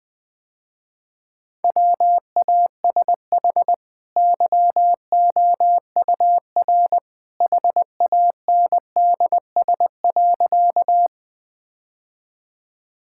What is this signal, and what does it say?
Morse code, heard as a keyed tone, about 21 wpm, 700 Hz. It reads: WASH YOUR HANDS.
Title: Easy Sentences with Hints - 20wpm